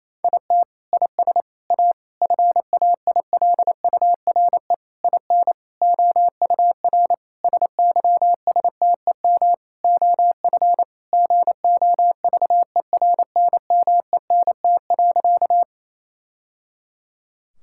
Morse code, heard as a keyed tone, about 28 wpm, 700 Hz. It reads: IT IS A FAILURE IN OUR SYSTEM OF GOVERNMENT.
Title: Sentences from Top 1000 Words - Review Part 6 - 28wpm